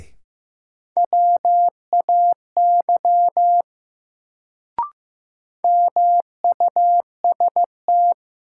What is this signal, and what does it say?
Morse code, heard as a keyed tone, about 15 wpm, 700 Hz. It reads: WAY MUST
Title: Sets of 2 Words - New Word Drill - 15wpm